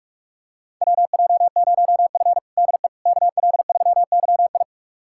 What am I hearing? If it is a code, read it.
Morse code, heard as a keyed tone, about 46 wpm, 700 Hz. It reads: WJ0F DE KL3YI
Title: USA Callsign DE Callsign 45wpm